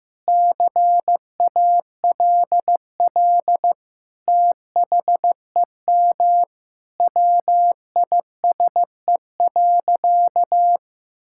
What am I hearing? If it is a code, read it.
Morse code, heard as a keyed tone, about 16 wpm, 700 Hz. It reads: CALL THEM WISE.
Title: Sentences from Top 1600 Words 15wpm